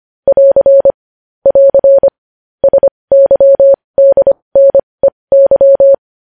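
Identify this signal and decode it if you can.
Morse code, heard as a keyed tone, about 25 wpm, spaced as for 20 wpm, 550 Hz. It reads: <AR> <AR> SYDNEY
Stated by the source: Morse code news headlines sent at 20 wpm.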